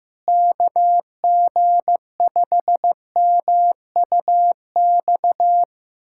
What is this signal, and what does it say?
Morse code, heard as a keyed tone, about 15 wpm, 700 Hz. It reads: KG5MUX
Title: USA Callsign DE Callsign - 6 Characters - 15wpm